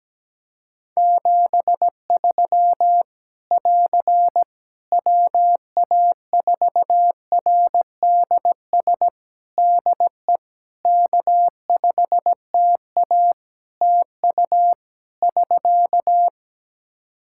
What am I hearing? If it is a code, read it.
Morse code, heard as a keyed tone, about 17 wpm, 700 Hz. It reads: 73 <AR> WA4RDS DE K5TA TU <SK>
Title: Practice QSO - Exchange 4 - Ending - 17wpm